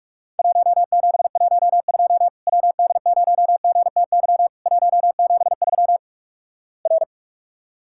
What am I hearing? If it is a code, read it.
Morse code, heard as a keyed tone, about 46 wpm, 700 Hz. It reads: 1712 WD0GTY 173 R
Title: Worked All Europe DX Contest - QTC Exchange - 45wpm